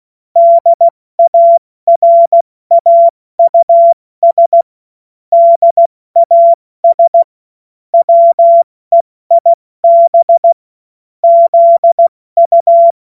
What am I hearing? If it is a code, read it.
Morse code, heard as a keyed tone, about 16 wpm, 680 Hz. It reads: DARAUS DAS WEIB ZU
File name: Morsekurs / Witze / Witz 21.November 2023 70 F80